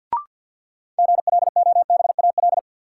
Morse code, heard as a keyed tone, about 53 wpm, 700 Hz. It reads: GLOBAL
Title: Sentences from Top 1200 Words - New Word Drill - 50wpm